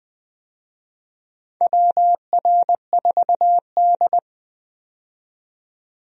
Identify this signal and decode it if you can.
Morse code, heard as a keyed tone, about 21 wpm, 700 Hz. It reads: WR4D
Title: USA Call Signs - 4 Characters - 20wpm